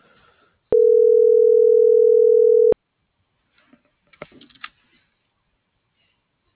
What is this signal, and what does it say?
Ringback tone (North America), 440 + 480 Hz.